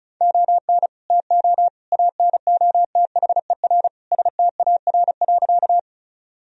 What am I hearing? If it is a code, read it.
Morse code, heard as a keyed tone, about 36 wpm, 700 Hz. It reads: ON TO ANOTHER STAR.
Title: Sentences from Top 1000 Words - Review Part 2 - 35wpm